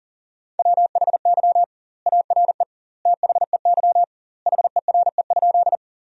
Morse code, heard as a keyed tone, about 40 wpm, 700 Hz. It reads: WHY ARE THEY HERE?
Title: 2x Word-Spacing - Sentences from Top 200 Words - 40wpm